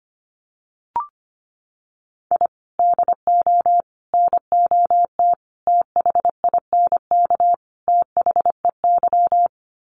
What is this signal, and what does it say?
Morse code, heard as a keyed tone, about 25 wpm, 700 Hz. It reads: I DO NOT THINK THEY
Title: Sets of 5 Words 25wpm